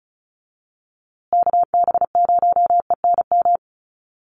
Morse code, heard as a keyed tone, about 34 wpm, 700 Hz. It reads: KB0ENM
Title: USA Call Signs Repeated 2x 35wpm